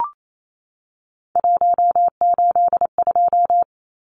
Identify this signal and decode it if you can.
Morse code, heard as a keyed tone, about 28 wpm, 700 Hz. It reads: 182
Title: Numbers - 3-Digit 28wpm